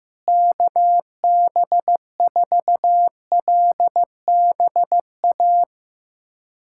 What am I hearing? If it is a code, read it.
Morse code, heard as a keyed tone, about 15 wpm, 700 Hz. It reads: KB4LBA
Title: USA Callsign DE Callsign 15wpm